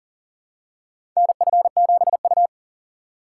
Morse code, heard as a keyed tone, about 40 wpm, 700 Hz. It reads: NF7U
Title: USA Call Signs - 4x Character Spacing - 40wpm